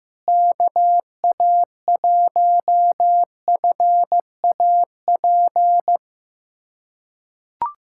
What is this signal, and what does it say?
Morse code, heard as a keyed tone, about 15 wpm, 700 Hz. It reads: KA1FAP
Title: USA Callsign DE Callsign - 6 Characters - 15wpm